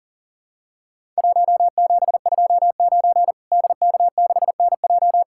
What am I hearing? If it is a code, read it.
Morse code, heard as a keyed tone, about 40 wpm, 700 Hz. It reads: 1729 DK6NJ
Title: Worked All Europe DX Contest - QTC Exchange - 40wpm